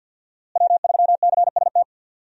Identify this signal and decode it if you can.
Morse code, heard as a keyed tone, about 51 wpm, 700 Hz. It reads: W3CST
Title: USA Call Signs - 5 Characters - Repeated 2x - 50wpm